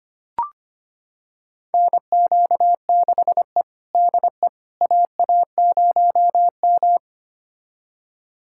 Morse code, heard as a keyed tone, about 25 wpm, 700 Hz. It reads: NQ6E DE AA0M
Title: USA Callsign DE Callsign - 4 Characters - 25wpm